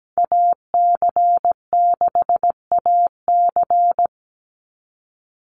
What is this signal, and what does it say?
Morse code, heard as a keyed tone, about 17 wpm, 700 Hz. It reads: AC6AC
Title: Call Signs - 17wpm